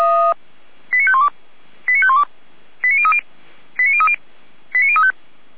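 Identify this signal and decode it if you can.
Five-tone selective call (ZVEI), heard as quick runs of five tones, tones of about 70 ms. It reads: 89512 89512 89030 89030 89035
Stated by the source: ZVEI five-tone selective call received by SDR, archived by the Orlamünde fire brigade (feuerwehr-orlamuende).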